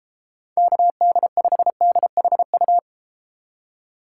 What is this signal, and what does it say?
Morse code, heard as a keyed tone, about 32 wpm, 690 Hz. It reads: KD5DHU